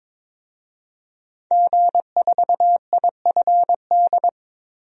Morse code, heard as a keyed tone, about 22 wpm, 700 Hz. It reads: G4IFD
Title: DXCC-100 Call Signs - 5 Characters - 22wpm